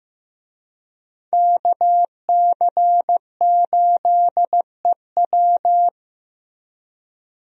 Morse code, heard as a keyed tone, about 15 wpm, 700 Hz. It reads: KC8EW
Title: USA Call Signs - 5 Characters - Repeated 3x - 15wpm